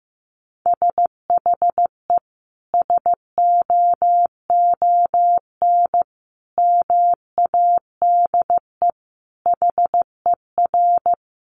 Morse code, heard as a keyed tone, about 15 wpm, 700 Hz. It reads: SHE SOON MADE HER
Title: Sentences from Top 1100 Words 15wpm